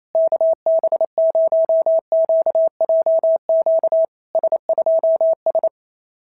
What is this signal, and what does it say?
Morse code, heard as a keyed tone, about 28 wpm, 650 Hz. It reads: KB0QJQ S2S
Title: SOTA QSO Practice 28wpm